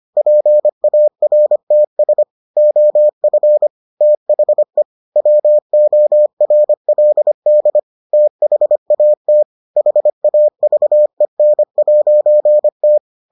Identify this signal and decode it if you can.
Morse code, heard as a keyed tone, about 25 wpm, 600 Hz. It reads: PARTS OF THE WORLD THAT HAVEN'T